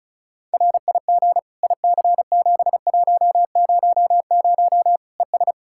Morse code, heard as a keyed tone, about 34 wpm, 700 Hz. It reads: RIG IC7100 ES
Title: Practice QSO - Exchange 2 - Info - 35wpm